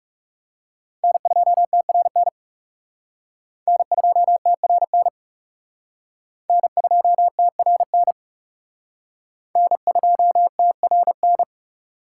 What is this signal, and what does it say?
Morse code, heard as a keyed tone, about 38 wpm, 700 Hz. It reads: N2TRN N2TRN N2TRN N2TRN